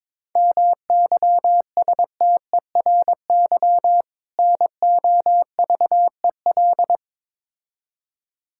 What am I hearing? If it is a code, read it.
Morse code, heard as a keyed tone, about 21 wpm, 700 Hz. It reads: MYSTERY NOVEL
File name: Difficult Words with Hints - 22wpm